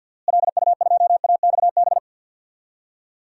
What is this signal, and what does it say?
Morse code, heard as a keyed tone, about 46 wpm, 700 Hz. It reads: LU2AXB
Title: DXCC-100 Call Signs - 6 Characters - Repeated 3x - 50wpm